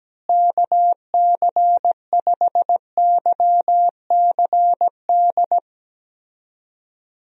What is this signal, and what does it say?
Morse code, heard as a keyed tone, about 17 wpm, 700 Hz. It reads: KC5YCD
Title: USA Call Signs - 6 Characters - Repeated 2x - 17wpm